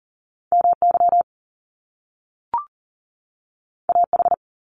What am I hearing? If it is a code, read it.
Morse code, heard as a keyed tone, about 40 wpm, 700 Hz. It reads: MY AH
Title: Top Two Letter Combinations in Words 40wpm